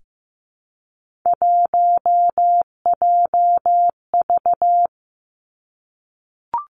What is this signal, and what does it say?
Morse code, heard as a keyed tone, about 15 wpm, 700 Hz. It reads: 1JV
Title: Three Challenging Letter-Number Combinations 15wpm